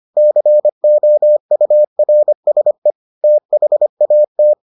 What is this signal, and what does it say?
Morse code, heard as a keyed tone, about 25 wpm, 600 Hz. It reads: COURSE THAT